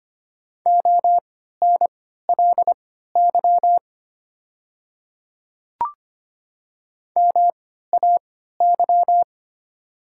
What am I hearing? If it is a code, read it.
Morse code, heard as a keyed tone, about 25 wpm, spaced as for 14 wpm, 690 Hz. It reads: ONLY MAY
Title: 3x Character Spacing - Top 200 Words - 25wpm